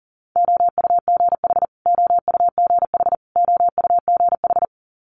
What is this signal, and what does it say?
Morse code, heard as a keyed tone, about 40 wpm, 690 Hz. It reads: OUGH OUGH OUGH
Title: Multi-letter Phonemes Repeated 3x 40wpm